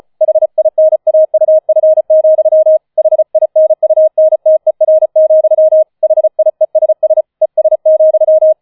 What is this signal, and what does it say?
Morse code, heard as a keyed tone, about 35 wpm, 630 Hz. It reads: HINAUF, HINUNTER, HIESS ES,